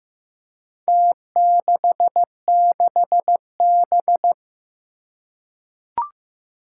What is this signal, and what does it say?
Morse code, heard as a keyed tone, about 15 wpm, 700 Hz. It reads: T66B
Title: Four Challenging Letter-Number Combinations 15wpm